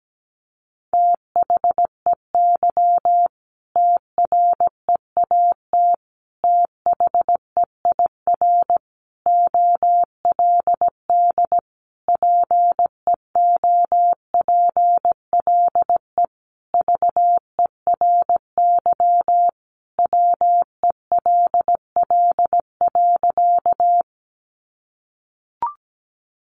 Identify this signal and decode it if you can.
Morse code, heard as a keyed tone, about 17 wpm, 700 Hz. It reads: THEY TREAT THEIR OLD PEOPLE VERY WELL.
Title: Sentences from Top 1300 Words 17wpm